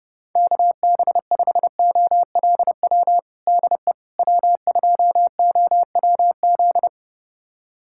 Morse code, heard as a keyed tone, about 32 wpm, 700 Hz. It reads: KB5OLW DE W2OWZ
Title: USA Callsign DE Callsign 30wpm